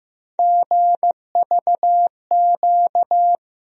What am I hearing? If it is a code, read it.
Morse code, heard as a keyed tone, about 15 wpm, 700 Hz. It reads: GVQ
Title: USA Call Sign Postfixes - 3 Letters 15wpm